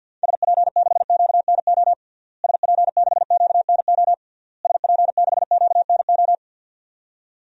Morse code, heard as a keyed tone, about 52 wpm, 700 Hz. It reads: SP6QNO SP6QNO SP6QNO
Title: DXCC-100 Call Signs - 6 Characters - Repeated 3x - 50wpm